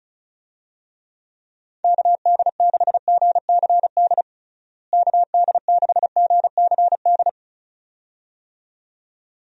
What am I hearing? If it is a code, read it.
Morse code, heard as a keyed tone, about 36 wpm, 700 Hz. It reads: KD6GCD KD6GCD
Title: USA Call Signs - 6 Characters - Repeated 2x - 35wpm